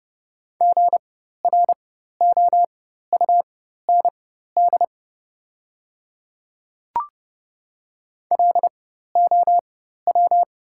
Morse code, heard as a keyed tone, about 30 wpm, spaced as for 14 wpm, 700 Hz. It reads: GROUND LOW
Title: 4x Character Spacing - Top 500 Words - 30wpm